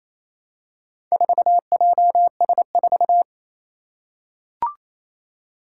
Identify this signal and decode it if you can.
Morse code, heard as a keyed tone, about 27 wpm, 700 Hz. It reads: 4JS4
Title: Four Challenging Letter-Number Combinations 28wpm